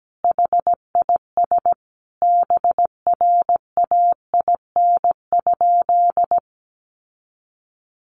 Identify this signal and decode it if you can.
Morse code, heard as a keyed tone, about 17 wpm, 700 Hz. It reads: HIS BRAIN?
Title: Sentences from Top 1100 Words 17wpm